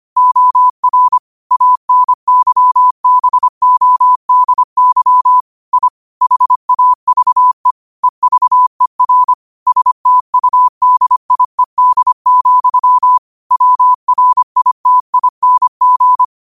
Morse code, heard as a keyed tone, about 25 wpm, 1000 Hz. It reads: OR ANYBODY I HAVE EVER STUDIED, WRITING
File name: qotdfiles / 25wpm